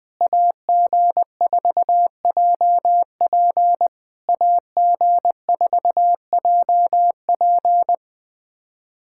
Morse code, heard as a keyed tone, about 20 wpm, 700 Hz. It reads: AG4JP AG4JP
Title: SOTA QSO Practice 20wpm